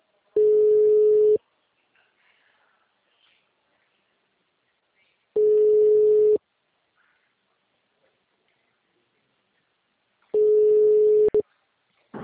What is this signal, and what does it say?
Ringback tone (Europe, 425 Hz).